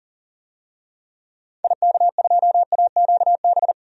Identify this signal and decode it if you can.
Morse code, heard as a keyed tone, about 38 wpm, 700 Hz. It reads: IK2AQB
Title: DXCC-100 Call Signs - 6 Characters - 40wpm